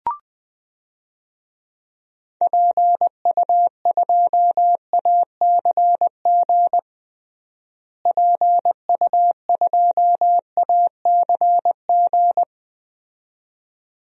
Morse code, heard as a keyed tone, about 21 wpm, 700 Hz. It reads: PU2ACG PU2ACG
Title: DXCC-100 Call Signs - 6 Characters - Repeated 2x - 20wpm